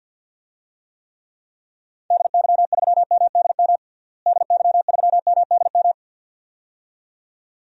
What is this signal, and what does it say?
Morse code, heard as a keyed tone, about 46 wpm, 700 Hz. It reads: DY3MDM DY3MDM
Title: DXCC-100 Call Signs - 6 Characters - Repeated 2x - 50wpm